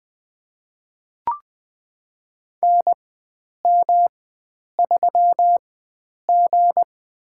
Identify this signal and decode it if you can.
Morse code, heard as a keyed tone, about 21 wpm, spaced as for 9 wpm, 700 Hz. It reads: NM3G
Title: USA Call Signs - 4 Characters - 4x Character Spacing - 20wpm